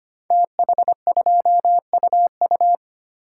Morse code, heard as a keyed tone, about 25 wpm, 700 Hz. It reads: TH2UU
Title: Code Groups - 5 Letter:Numbers - 25wpm